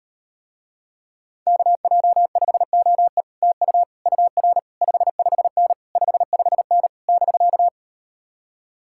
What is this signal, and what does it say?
Morse code, heard as a keyed tone, about 38 wpm, 700 Hz. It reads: KJ5OE TU UR 55N 55N <BK>